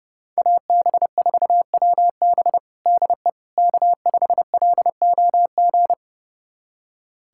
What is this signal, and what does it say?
Morse code, heard as a keyed tone, about 30 wpm, 700 Hz. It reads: AB4WB DE K5LOG
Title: USA Callsign DE Callsign - 5 Characters - 30wpm